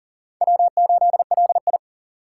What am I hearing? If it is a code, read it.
Morse code, heard as a keyed tone, about 40 wpm, 700 Hz. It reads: W8LI
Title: USA Call Signs - 4x Character Spacing - 40wpm